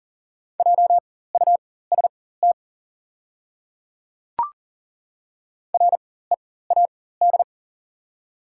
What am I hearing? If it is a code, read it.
Morse code, heard as a keyed tone, about 40 wpm, spaced as for 19 wpm, 700 Hz. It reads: JUST READ
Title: 4x Character Spacing - Top 200 Words - 40wpm